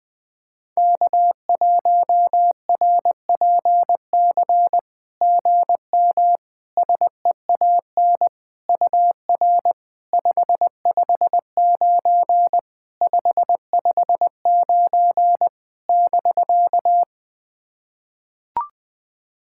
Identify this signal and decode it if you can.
Morse code, heard as a keyed tone, about 21 wpm, 700 Hz. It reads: K1RPC GM SEAN UR 559 559 <BK>
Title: SOTA QSO Practice 20wpm